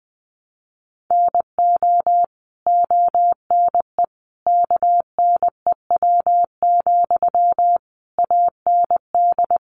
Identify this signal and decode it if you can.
Morse code, heard as a keyed tone, about 20 wpm, 700 Hz. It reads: NO ONE KNEW, AND
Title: Sentences from Top 1400 Words 20wpm